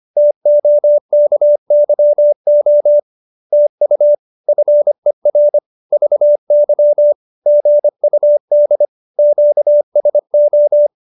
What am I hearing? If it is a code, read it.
Morse code, heard as a keyed tone, about 25 wpm, 590 Hz. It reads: TOKYO TU FER VY GUD QSO